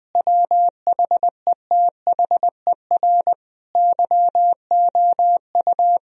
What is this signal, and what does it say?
Morse code, heard as a keyed tone, about 20 wpm, 700 Hz. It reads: WHETHER YOU
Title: Sentences from Top 1000 Words - Review Part 3 - 20wpm